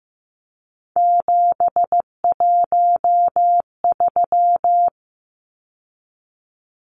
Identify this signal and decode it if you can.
Morse code, heard as a keyed tone, about 15 wpm, 700 Hz. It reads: 713
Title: Numbers - 3-Digit 15wpm